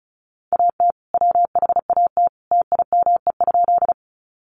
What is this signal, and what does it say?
Morse code, heard as a keyed tone, about 34 wpm, 700 Hz. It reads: AT WHAT TIME?
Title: Sentences from Top 100 Words 35wpm